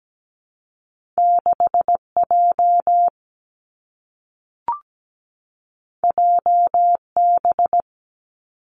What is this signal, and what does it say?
Morse code, heard as a keyed tone, about 17 wpm, 700 Hz. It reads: 6J JB
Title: Two Challenging Letter-Number Combinations 17wpm